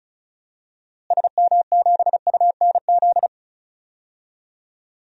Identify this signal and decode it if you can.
Morse code, heard as a keyed tone, about 34 wpm, 700 Hz. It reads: SM7UNZ
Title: DXCC-100 Call Signs - 6 Characters - Repeated 2x - 35wpm